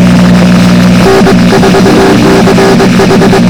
Morse code, heard as a keyed tone, about 21 wpm, 390 Hz, with noise. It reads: N4CH
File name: n3db / YV0Dpile